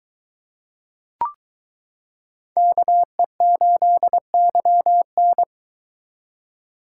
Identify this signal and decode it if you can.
Morse code, heard as a keyed tone, about 22 wpm, 700 Hz. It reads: KE8YN